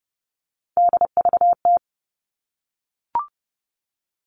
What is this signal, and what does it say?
Morse code, heard as a keyed tone, about 30 wpm, 700 Hz. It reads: DVT
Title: All Cut Numbers - 3-Digit 30wpm